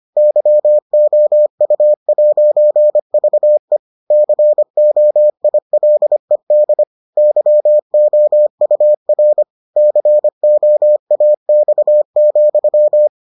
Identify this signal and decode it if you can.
Morse code, heard as a keyed tone, about 25 wpm, 600 Hz. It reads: YOU'VE COILED YOUR COAX,